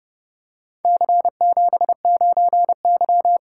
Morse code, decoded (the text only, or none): C79Y